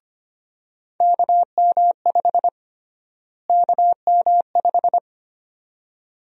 Morse code, heard as a keyed tone, about 25 wpm, 700 Hz. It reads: KM5 KM5
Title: USA Call Sign Prefixes - Repeated 2x - 25wpm